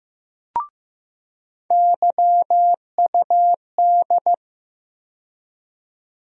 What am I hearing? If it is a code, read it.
Morse code, heard as a keyed tone, about 15 wpm, 700 Hz. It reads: YUD
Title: USA Call Sign Postfixes - 3 Letters 15wpm